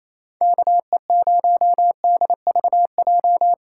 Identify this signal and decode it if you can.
Morse code, heard as a keyed tone, about 35 wpm, 700 Hz. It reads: KE0DVJ
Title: Speed Racing - USA 2x3 Call Signs 28wpm